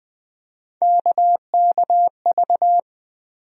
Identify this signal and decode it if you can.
Morse code, heard as a keyed tone, about 20 wpm, 700 Hz. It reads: KKV
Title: USA Call Sign Postfixes - 3 Letters 20wpm